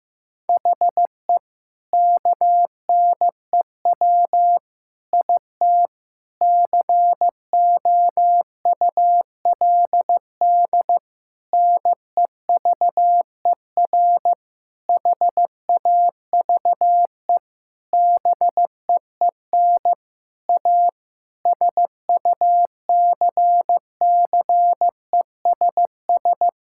Morse code, heard as a keyed tone, about 16 wpm, 700 Hz. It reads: HE KNEW IT COULD NEVER HAVE BEEN A SUCCESS
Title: Sentences from Top 1000 Words - Review Part 6 - 15wpm